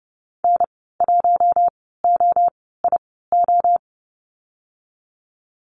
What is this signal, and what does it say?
Morse code, heard as a keyed tone, about 30 wpm, spaced as for 17 wpm, 700 Hz. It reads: N1OIO
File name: Lesson 14g - Number 2 - Call Signs 30wpm